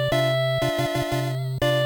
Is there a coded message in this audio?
No.